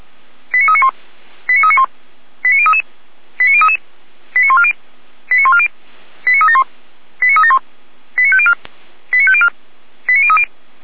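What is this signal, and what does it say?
Five-tone selective call (ZVEI), heard as quick runs of five tones, tones of about 70 ms. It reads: 89391 89391 89033 89033 89140 89140 89371 89371 89504 89504 89030
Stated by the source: ZVEI five-tone call 89071 received by SDR, archived by Feuerwehr Orlamünde (fire brigade).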